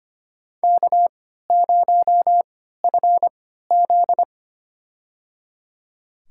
Morse code, heard as a keyed tone, about 25 wpm, spaced as for 14 wpm, 700 Hz. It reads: K0FZ
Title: USA Call Signs - 4 Characters - 3x Character Spacing - 25wpm